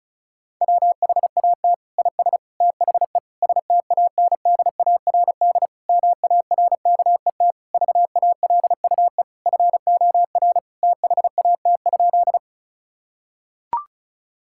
Morse code, heard as a keyed tone, about 34 wpm, 700 Hz. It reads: WHAT IS THE STANDARD MARKET VALUE FOR THAT?
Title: Sentences from Top 1000 Words - Review Part 6 - 35wpm